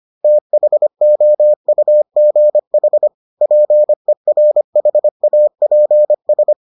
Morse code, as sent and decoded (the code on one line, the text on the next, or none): - .... --- ..- --. .... / .--. .-. .... .- .--. ...
THOUGH PERHAPS